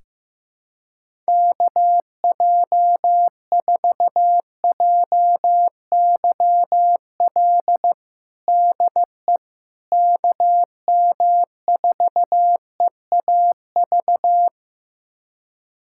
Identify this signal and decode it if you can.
Morse code, heard as a keyed tone, about 16 wpm, 700 Hz. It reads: KJ4JYL DE KM4EAV
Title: USA Callsign DE Callsign 15wpm